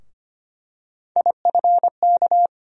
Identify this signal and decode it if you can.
Morse code, heard as a keyed tone, about 25 wpm, 700 Hz. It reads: IFK
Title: USA Call Sign Postfixes - 3 Letters 25wpm